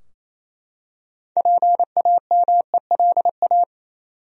Morse code, heard as a keyed tone, about 27 wpm, 700 Hz. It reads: PAMELA
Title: USA Common Names - 28wpm